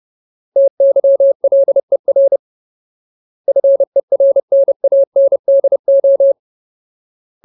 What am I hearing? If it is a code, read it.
Morse code, heard as a keyed tone, about 30 wpm, 550 Hz. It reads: TYLER FERNANDO
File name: Ultimate Word List / 30WPM Ultimate